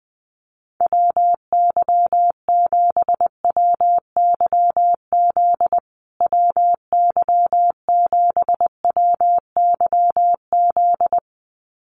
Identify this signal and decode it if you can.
Morse code, heard as a keyed tone, about 20 wpm, 700 Hz. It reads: WY7WYZ WY7WYZ
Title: USA Call Signs Repeated 2x 20wpm